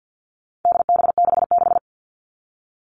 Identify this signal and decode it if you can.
Morse code, heard as a keyed tone, about 50 wpm, 700 Hz. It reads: DB66
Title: All Cut Numbers - 4-Digit 50wpm